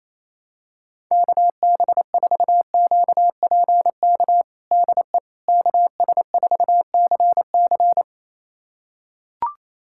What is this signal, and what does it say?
Morse code, heard as a keyed tone, about 27 wpm, 700 Hz. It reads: KB4QPK DE KS4CC
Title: USA Callsign DE Callsign 28wpm